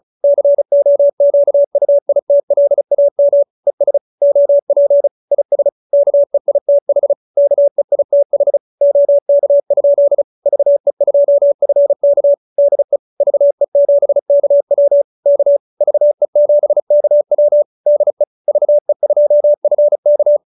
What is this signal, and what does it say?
Morse code, heard as a keyed tone, about 34 wpm, 570 Hz. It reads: COQUITLAM ES OP IS KEITH KEITH OK? VE2FK DE VE7KW K VE7KW DE VE2FK
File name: QSO314 35